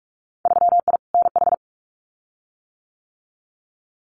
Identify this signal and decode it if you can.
Morse code, heard as a keyed tone, about 44 wpm, 700 Hz. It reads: 3I NH
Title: Winter Field Day Exchanges Encore 45wpm